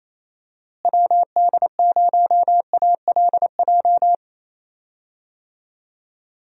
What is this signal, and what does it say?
Morse code, heard as a keyed tone, about 28 wpm, 700 Hz. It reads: WD0ALJ
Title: USA Call Signs - 6 Characters - Repeated 3x - 28wpm